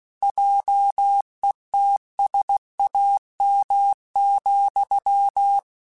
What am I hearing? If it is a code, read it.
Morse code, heard as a keyed tone, about 16 wpm, 800 Hz. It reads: JETSAM,
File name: Misc / Misc-mp3s / Word Pairs @15